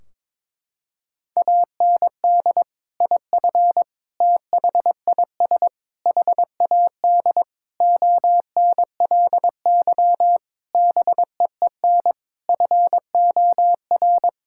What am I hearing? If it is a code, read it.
Morse code, heard as a keyed tone, about 21 wpm, 700 Hz. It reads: AND IF THIS HAD ONLY BEEN FOR
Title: Sentences from Top 1600 Words 22wpm